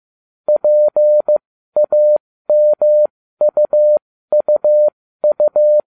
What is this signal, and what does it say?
Morse code, heard as a keyed tone, about 15 wpm, spaced as for 13 wpm, 600 Hz. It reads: PAMUUU
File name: cw / ubacw07 1